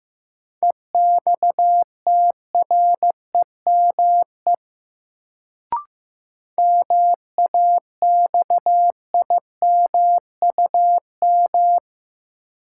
Morse code, heard as a keyed tone, about 15 wpm, 700 Hz. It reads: EXTREME MAXIMUM
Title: Sentences from Top 1300 Words - New Word Drill - 15wpm